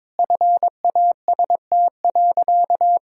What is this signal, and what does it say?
Morse code, heard as a keyed tone, about 21 wpm, 700 Hz. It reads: FAST.
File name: practice / 22wpm / Sentences with 2-Word Phrases - 22wpm